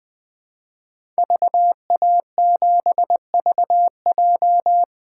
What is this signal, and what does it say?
Morse code, heard as a keyed tone, about 21 wpm, 700 Hz. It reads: VA7VJ
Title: 2023 SOTA Chaser Call Signs - 20wpm